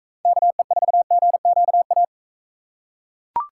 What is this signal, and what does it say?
Morse code, heard as a keyed tone, about 41 wpm, 700 Hz. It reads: KE4GQA